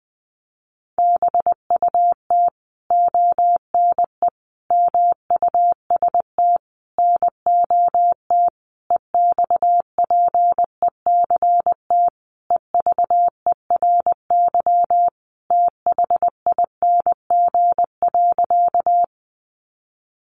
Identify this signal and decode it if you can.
Morse code, heard as a keyed tone, about 20 wpm, 700 Hz. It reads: BUT ONE MUST NOT EXPECT EVERYTHING.
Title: Sentences from Top 1100 Words 20wpm